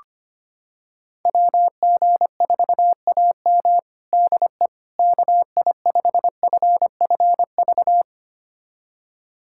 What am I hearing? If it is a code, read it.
Morse code, heard as a keyed tone, about 25 wpm, 700 Hz. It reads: WG4AM DE KI5FFV
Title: USA Callsign DE Callsign 25wpm